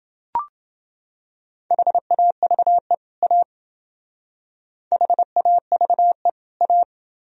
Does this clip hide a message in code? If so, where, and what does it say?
Yes: Morse code from 1.7 to 6.8 s, heard as a keyed tone, about 28 wpm, 700 Hz. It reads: HAVE A HAVE A